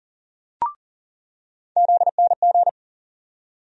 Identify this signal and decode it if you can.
Morse code, heard as a keyed tone, about 39 wpm, 700 Hz. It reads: ZNG
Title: USA Call Sign Postfixes - 3 Letters 40wpm